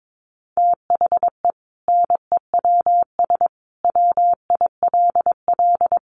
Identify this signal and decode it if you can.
Morse code, heard as a keyed tone, about 22 wpm, 700 Hz. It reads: THE NEWS WILL
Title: Sentences from Top 1200 Words 22wpm